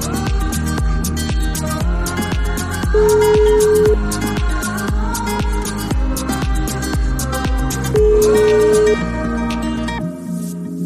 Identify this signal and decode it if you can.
Ringback tone (Europe, 425 Hz), with noise.